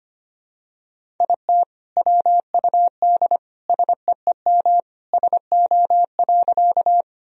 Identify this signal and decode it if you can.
Morse code, heard as a keyed tone, about 25 wpm, 700 Hz. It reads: IT WUD SEEM SO.
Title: Sentences from Top 300 Words and Abbreviations 25wpm